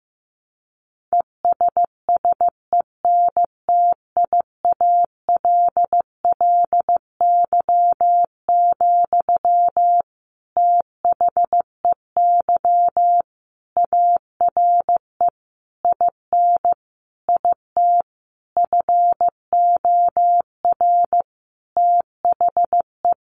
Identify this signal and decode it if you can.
Morse code, heard as a keyed tone, about 15 wpm, 700 Hz. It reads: ESSENTIALLY, THEY ARE IN IT FOR THE
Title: Sentences from Top 1300 Words 15wpm